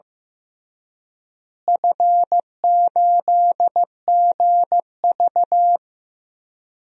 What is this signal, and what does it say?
Morse code, heard as a keyed tone, about 15 wpm, 700 Hz. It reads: F8GV
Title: DXCC-100 Call Signs - 4 Characters - Repeated 2x - 15wpm